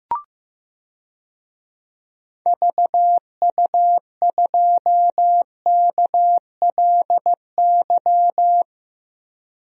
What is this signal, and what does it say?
Morse code, heard as a keyed tone, about 15 wpm, 700 Hz. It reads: VU2KLY
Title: DXCC-100 Call Signs - 6 Characters - 15wpm